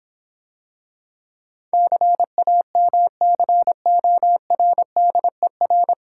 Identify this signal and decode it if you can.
Morse code, heard as a keyed tone, about 26 wpm, 700 Hz. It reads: CAMCORDER